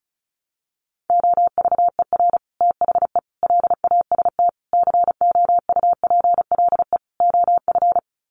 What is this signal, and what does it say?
Morse code, heard as a keyed tone, about 34 wpm, 700 Hz. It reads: OVER THE LAST COUPLE OF
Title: Sets of 5 Words Encore 35wpm